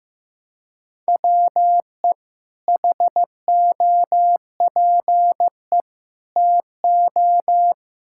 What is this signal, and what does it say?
Morse code, heard as a keyed tone, about 16 wpm, 700 Hz. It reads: WE HOPE TO
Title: Sentences from Top 1000 Words - Review Part 6 - 15wpm